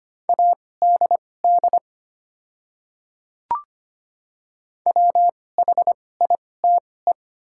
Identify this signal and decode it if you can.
Morse code, heard as a keyed tone, about 25 wpm, spaced as for 18 wpm, 700 Hz. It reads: ADD WHITE
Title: 2x Character Spacing - Top 500 Words - 25wpm